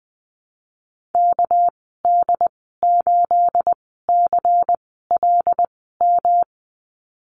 Morse code, heard as a keyed tone, about 19 wpm, spaced as for 14 wpm, 700 Hz. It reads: KD8CLM
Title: 231 Lesson 37g - Letter x - Call Signs 20wpm